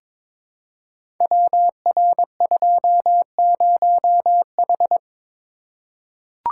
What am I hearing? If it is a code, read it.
Morse code, heard as a keyed tone, about 22 wpm, 700 Hz. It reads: WR20H
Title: Code Groups - 5 Letter:Numbers - 22wpm